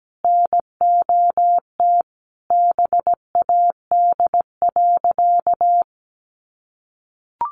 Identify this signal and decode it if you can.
Morse code, heard as a keyed tone, about 17 wpm, 700 Hz. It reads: NOT BAD.
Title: Sentences from Top 900 Words 17wpm